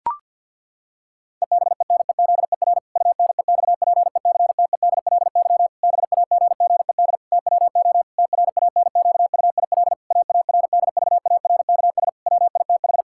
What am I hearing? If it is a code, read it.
Morse code, heard as a keyed tone, about 52 wpm, 700 Hz. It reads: EBENEZER UNEXPECTEDLY BAGGED TWO TRANQUIL AARDVARKS WITH
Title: Pangram Challenge v2 - 50wpm